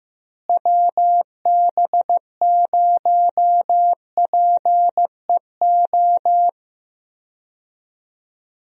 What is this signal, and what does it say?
Morse code, heard as a keyed tone, about 15 wpm, 700 Hz. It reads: WB0PEO
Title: USA Call Signs - 6 Characters - Repeated 2x - 15wpm